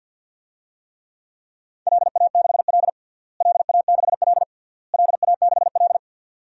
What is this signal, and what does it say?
Morse code, heard as a keyed tone, about 47 wpm, 700 Hz. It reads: LA6L LA6L LA6L
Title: DXCC-100 Call Signs - 4 Characters - Repeated 3x - 50wpm